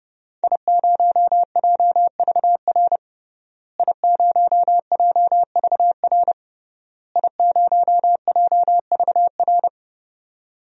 Morse code, heard as a keyed tone, about 30 wpm, 700 Hz. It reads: I0JVR I0JVR I0JVR
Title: DXCC-100 Call Signs - 5 Characters - Repeated 3x - 30wpm